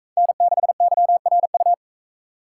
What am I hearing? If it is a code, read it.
Morse code, heard as a keyed tone, about 44 wpm, 700 Hz. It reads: N6YRU